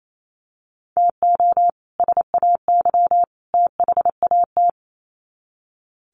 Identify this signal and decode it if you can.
Morse code, heard as a keyed tone, about 28 wpm, 700 Hz. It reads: TO SAY THAT
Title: Sets of 3 Words 28wpm